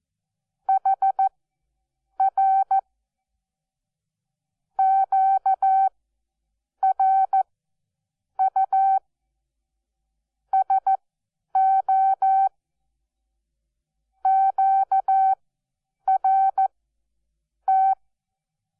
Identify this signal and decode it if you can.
Morse code, heard as a keyed tone, about 14 wpm, spaced as for 7 wpm, 770 Hz. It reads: HR QRU SO QRT
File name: Onlinelehrgang / morsen / DJ4UF Morsen12